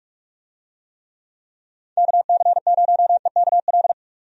Morse code, heard as a keyed tone, about 41 wpm, 700 Hz. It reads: KC0EKL
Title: USA Call Signs - 6 Characters - 45wpm